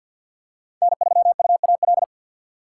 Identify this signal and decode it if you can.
Morse code, heard as a keyed tone, about 53 wpm, 700 Hz. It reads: N3UAL
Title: USA Call Signs - 5 Characters - 6x Character Spacing - 50wpm